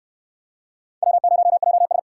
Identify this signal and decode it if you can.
Morse code, heard as a keyed tone, about 59 wpm, 700 Hz. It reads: W0JS